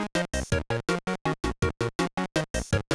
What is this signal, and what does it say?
There is no code.